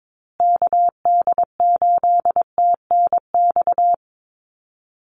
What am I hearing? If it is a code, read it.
Morse code, heard as a keyed tone, about 22 wpm, 700 Hz. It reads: KD8TNX
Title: USA Call Signs Repeated 3x 22wpm